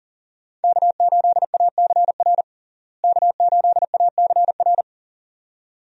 Morse code, heard as a keyed tone, about 37 wpm, 700 Hz. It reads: K8ACR K8ACR
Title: USA Call Signs - 5 Characters - Repeated 2x - 40wpm